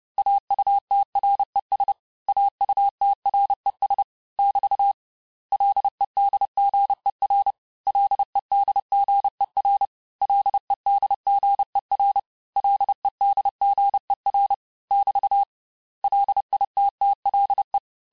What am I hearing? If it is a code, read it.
Morse code, heard as a keyed tone, about 30 wpm, 800 Hz. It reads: AUTRES AUTRES <BT> LEDGER LEDGER LEDGER LEDGER <BT> LITTLE
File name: cours-du-mercredi / audios / cours-854-mots-vitesse-28-mpm